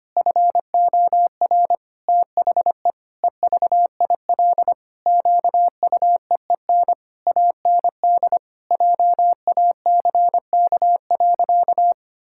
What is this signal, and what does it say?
Morse code, heard as a keyed tone, about 25 wpm, 700 Hz. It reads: FOR THE EVIL QUEEN AND JACK.
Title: Pangram Challenge v2 - 25wpm